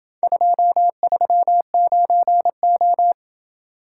Morse code, heard as a keyed tone, about 26 wpm, 700 Hz. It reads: 239O